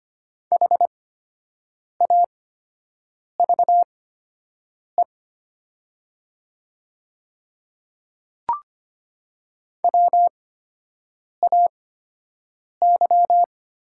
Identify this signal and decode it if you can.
Morse code, heard as a keyed tone, about 25 wpm, spaced as for 7 wpm, 700 Hz. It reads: HAVE WAY
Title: 8x Character Spacing - Top 100 Words - 25wpm